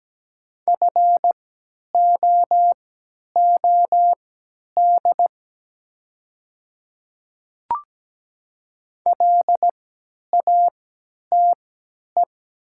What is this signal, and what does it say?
Morse code, heard as a keyed tone, about 17 wpm, spaced as for 10 wpm, 700 Hz. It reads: FOOD LATE
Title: 3x Character Spacing - Top 500 Words - 17wpm